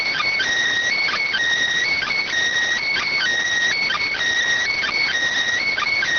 Emergency Alert System (SAME) tones, heard as harsh buzzing bursts. It is heard from 1.0 to 5.0 s.